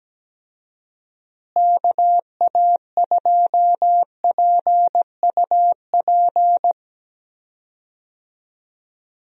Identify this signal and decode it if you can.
Morse code, heard as a keyed tone, about 17 wpm, 700 Hz. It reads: KA2PUP
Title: USA Call Signs - 6 Characters - Repeated 3x - 17wpm